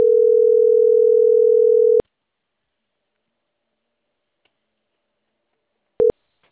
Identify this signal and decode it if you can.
Ringback tone (North America), 440 + 480 Hz.